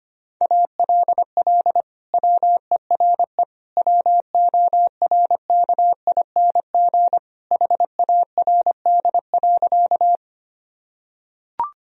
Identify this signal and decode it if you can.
Morse code, heard as a keyed tone, about 25 wpm, 700 Hz. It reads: ALL WERE WORKING HARD.
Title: Sentences from Top 1000 Words - Review Part 4 - 25wpm